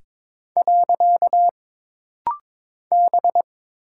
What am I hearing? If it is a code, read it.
Morse code, heard as a keyed tone, about 22 wpm, 700 Hz. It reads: . B
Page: This rapid-fire timing is geared to push your ICR (Instant Character Recognition) skills.